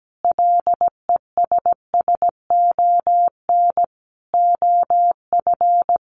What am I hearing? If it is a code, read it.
Morse code, heard as a keyed tone, about 17 wpm, 700 Hz. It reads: LESSON OF
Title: Sentences from Top 1200 Words 17wpm